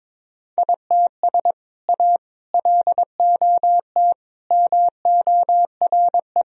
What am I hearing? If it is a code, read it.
Morse code, heard as a keyed tone, about 22 wpm, 700 Hz. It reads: ITS A LOT MORE
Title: Sentences from Top 1000 Words - Review Part 3 - 22wpm